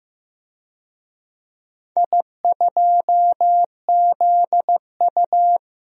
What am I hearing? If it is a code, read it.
Morse code, heard as a keyed tone, about 15 wpm, 700 Hz. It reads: I2ZU
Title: DXCC-100 Call Signs - 4 Characters - Repeated 3x - 15wpm